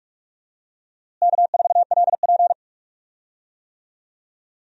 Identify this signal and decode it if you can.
Morse code, heard as a keyed tone, about 46 wpm, 700 Hz. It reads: K4LP
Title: USA Call Signs - 4 Characters - Repeated 2x - 45wpm